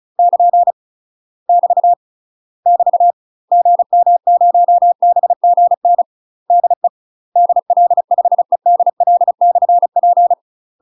Morse code, heard as a keyed tone, about 36 wpm, 700 Hz. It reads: <KN> <BT> <BT> GM0BGN DE DL5EDL/P
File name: UK MASTER QSOs TRAINING / 35WPM / UK MASTER QSOs 0011